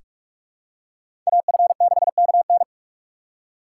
Morse code, heard as a keyed tone, about 46 wpm, 700 Hz. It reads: AF6KN
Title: USA Call Signs - 5 Characters - 4x Character Spacing - 45wpm